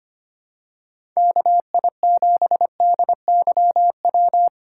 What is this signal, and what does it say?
Morse code, heard as a keyed tone, about 25 wpm, 700 Hz. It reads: KI7DYW
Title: USA Call Signs - 6 Characters - 4x Character Spacing - 25wpm